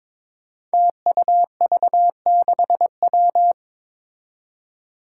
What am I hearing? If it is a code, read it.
Morse code, heard as a keyed tone, about 21 wpm, 700 Hz. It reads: TUV6W
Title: Five Challenging Letter-Number Combinations 22wpm